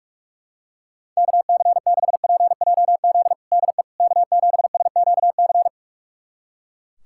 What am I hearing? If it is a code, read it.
Morse code, heard as a keyed tone, about 41 wpm, 700 Hz. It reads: KC6PJZ DE K7SQC
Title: USA Callsign DE Callsign 45wpm